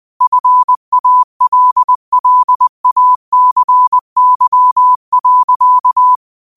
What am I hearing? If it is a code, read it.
Morse code, heard as a keyed tone, about 20 wpm, 1000 Hz. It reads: FALLACY.
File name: qotdfiles / 20wpm